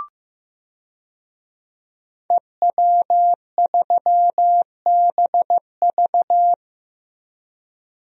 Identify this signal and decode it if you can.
Morse code, heard as a keyed tone, about 15 wpm, 700 Hz. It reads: EW3BV
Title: DXCC-100 Call Signs - 5 Characters - Repeated 2x - 15wpm